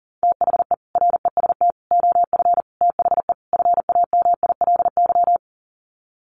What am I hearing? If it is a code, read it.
Morse code, heard as a keyed tone, about 40 wpm, 700 Hz. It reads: THE REST OF THE FAMILY
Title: Sets of 5 Words Encore2 40wpm